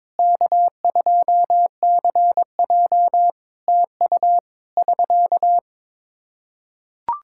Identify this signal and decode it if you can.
Morse code, heard as a keyed tone, about 22 wpm, 700 Hz. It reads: K2CJ TU <SK>
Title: K1USN SST QSO Practice 22wpm